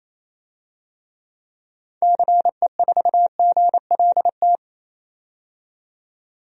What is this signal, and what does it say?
Morse code, heard as a keyed tone, about 28 wpm, 700 Hz. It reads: CE4GLT
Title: DXCC-100 Call Signs - 6 Characters - 28wpm